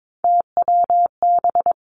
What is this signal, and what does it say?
Morse code, heard as a keyed tone, about 22 wpm, 700 Hz. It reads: TWB
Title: Three Challenging Letter-Number Combinations 22wpm